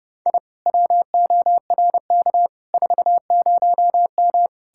Morse code, heard as a keyed tone, about 36 wpm, 700 Hz. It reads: I WORK 40M
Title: Speed-Racing Common QSO Phrases Encore - 30wpm